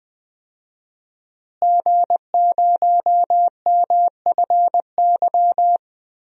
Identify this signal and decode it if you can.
Morse code, heard as a keyed tone, about 20 wpm, 700 Hz. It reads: G0MFY
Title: DXCC-100 Call Signs - 5 Characters - Repeated 3x - 20wpm